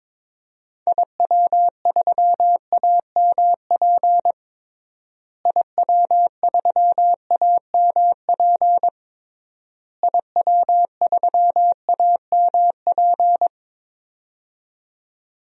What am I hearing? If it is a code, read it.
Morse code, heard as a keyed tone, about 21 wpm, 700 Hz. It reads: IW3AMP IW3AMP IW3AMP
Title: DXCC-100 Call Signs - 6 Characters - Repeated 3x - 22wpm